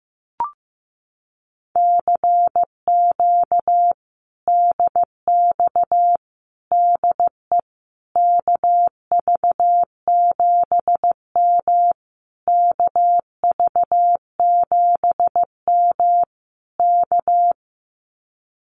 Morse code, heard as a keyed tone, about 15 wpm, 700 Hz. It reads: CQ DX DE KV7M KV7M K
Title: Common QSO Phrases Encore 15wpm